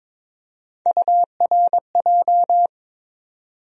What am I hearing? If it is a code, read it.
Morse code, heard as a keyed tone, about 21 wpm, 700 Hz. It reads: URJ
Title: USA Call Sign Postfixes - 3 Letters 22wpm